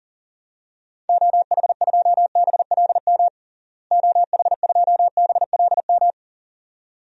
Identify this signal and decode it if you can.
Morse code, heard as a keyed tone, about 40 wpm, 700 Hz. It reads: OH2BLM OH2BLM
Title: DXCC-100 Call Signs - 6 Characters - Repeated 2x - 40wpm